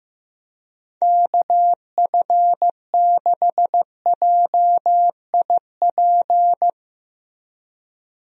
Morse code, heard as a keyed tone, about 15 wpm, 700 Hz. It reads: KF6JIP
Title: USA Call Signs - 6 Characters - Repeated 3x - 15wpm